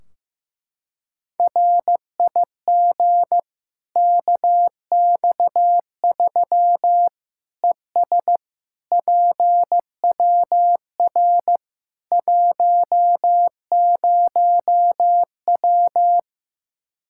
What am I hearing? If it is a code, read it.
Morse code, heard as a keyed tone, about 15 wpm, 700 Hz. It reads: RIG KX3 ES PWR 10W
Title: Practice QSO - Exchange 2 - Info - 15wpm